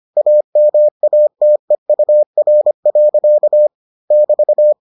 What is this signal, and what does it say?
Morse code, heard as a keyed tone, about 25 wpm, 600 Hz. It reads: AMATEUR. <BT>